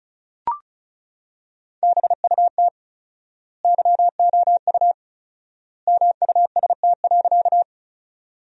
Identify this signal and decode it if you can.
Morse code, heard as a keyed tone, about 34 wpm, 700 Hz. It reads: BUT YOU MUST.
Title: 4x Word-Spacing - Sentences from Top 200 Words - 35wpm